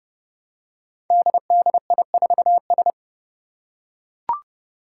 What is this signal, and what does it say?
Morse code, heard as a keyed tone, about 30 wpm, 700 Hz. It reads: DDI4S
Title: Five Challenging Letter-Number Combinations 30wpm